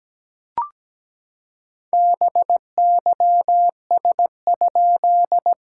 Morse code, heard as a keyed tone, about 17 wpm, 700 Hz. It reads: BYS?
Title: SOTA QSO Practice 17wpm